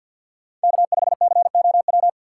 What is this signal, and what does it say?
Morse code, heard as a keyed tone, about 50 wpm, 700 Hz. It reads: K5COW
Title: USA Call Signs - 5 Characters - 3x Character Spacing - 50wpm